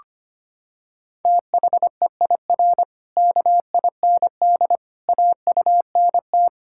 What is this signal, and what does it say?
Morse code, heard as a keyed tone, about 25 wpm, 700 Hz. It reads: THEIR KIND AUNT
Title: Pangram Challenge v2 - 25wpm